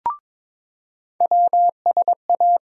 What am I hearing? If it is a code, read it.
Morse code, heard as a keyed tone, about 22 wpm, 700 Hz. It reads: WSA
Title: USA Call Sign Postfixes - 3 Letters 22wpm